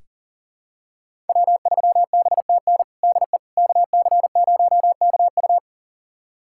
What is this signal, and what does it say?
Morse code, heard as a keyed tone, about 40 wpm, 700 Hz. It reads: W3BTN DE KC0KU
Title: USA Callsign DE Callsign - 5 Characters - 40wpm